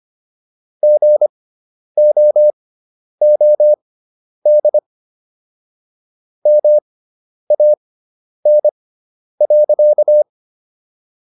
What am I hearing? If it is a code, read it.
Morse code, heard as a keyed tone, about 25 wpm, spaced as for 10 wpm, 600 Hz. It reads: GOOD MAN.